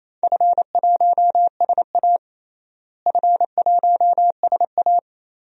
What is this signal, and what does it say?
Morse code, heard as a keyed tone, about 28 wpm, 700 Hz. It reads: F1SA F1SA
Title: DXCC-100 Call Signs - 4 Characters - Repeated 3x - 28wpm